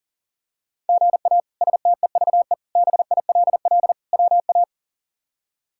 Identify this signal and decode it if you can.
Morse code, heard as a keyed tone, about 40 wpm, 700 Hz. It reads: GA STEVE BILL WA
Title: K1USN SST QSO Practice - 40wpm